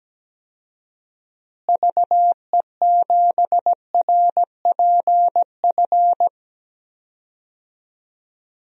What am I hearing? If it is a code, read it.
Morse code, heard as a keyed tone, about 17 wpm, 700 Hz. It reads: VE7RPF
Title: DXCC-100 Call Signs - 6 Characters - Repeated 2x - 17wpm